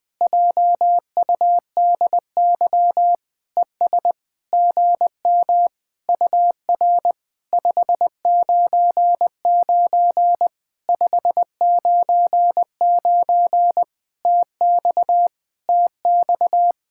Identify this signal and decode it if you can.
Morse code, heard as a keyed tone, about 20 wpm, 700 Hz. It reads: JUDY ES GM UR 599 599 TX TX
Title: POTA QSO Practice - 20wpm